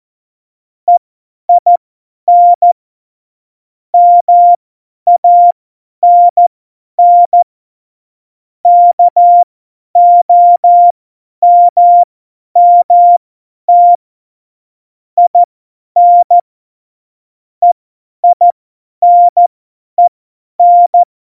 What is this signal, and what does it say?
Morse code, heard as a keyed tone, about 14 wpm, spaced as for 10 wpm, 700 Hz. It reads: EIN MANN KOMMT IN EINEN
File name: Morsekurs / Witze / Witz 29. Januar 2019, 50 F70